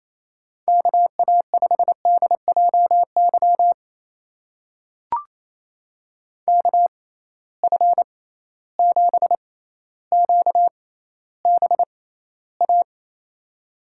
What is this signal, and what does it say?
Morse code, heard as a keyed tone, about 28 wpm, spaced as for 14 wpm, 700 Hz. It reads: KA5DJY KF7QBA
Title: USA Call Signs - 6x Character Spacing - 28wpm